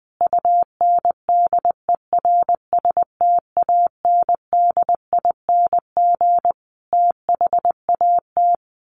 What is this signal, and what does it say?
Morse code, heard as a keyed tone, about 20 wpm, 700 Hz. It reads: UNDERSTANDING THAT
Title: Sentences from Top 1500 Words 20wpm